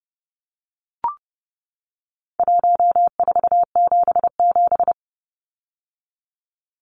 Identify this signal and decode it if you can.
Morse code, heard as a keyed tone, about 30 wpm, 700 Hz. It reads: 1477
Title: Numbers - Contest Style - 4-Digit 30wpm